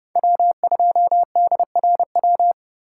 Morse code, heard as a keyed tone, about 32 wpm, 700 Hz. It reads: W2DRW